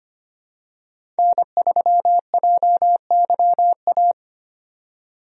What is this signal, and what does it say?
Morse code, heard as a keyed tone, about 25 wpm, 700 Hz. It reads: N3JYA
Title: USA Call Signs - 5 Characters - 25wpm